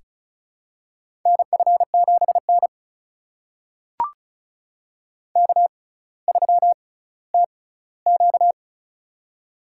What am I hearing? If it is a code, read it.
Morse code, heard as a keyed tone, about 36 wpm, spaced as for 18 wpm, 700 Hz. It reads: NF7N K3TQ
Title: USA Call Signs - 4 Characters - 6x Character Spacing - 35wpm